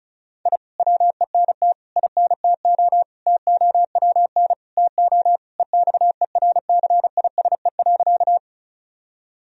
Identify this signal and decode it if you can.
Morse code, heard as a keyed tone, about 34 wpm, 700 Hz. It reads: I WENT INTO TOWN TO EXERCISE.
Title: Sentences from Top 1000 Words - Review Part 4 - 35wpm